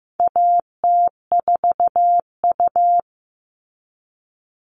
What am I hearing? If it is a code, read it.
Morse code, heard as a keyed tone, about 15 wpm, 700 Hz. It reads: AT4U
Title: All Cut Numbers - 4-Digit 15wpm